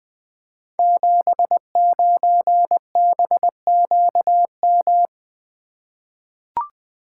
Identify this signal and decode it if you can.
Morse code, heard as a keyed tone, about 20 wpm, 700 Hz. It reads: 79BQM
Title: Code Groups - 5 Letter:Numbers - 20wpm